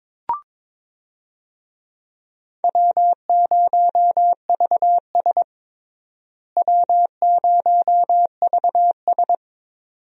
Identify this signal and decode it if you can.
Morse code, heard as a keyed tone, about 21 wpm, 700 Hz. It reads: W0VS W0VS
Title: USA Call Signs - 4 Characters - Repeated 2x - 22wpm